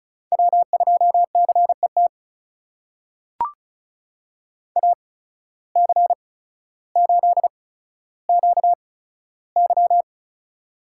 Morse code, heard as a keyed tone, about 36 wpm, spaced as for 15 wpm, 700 Hz. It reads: W2CET AC8QY
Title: USA Call Signs - 8x Character Spacing - 35wpm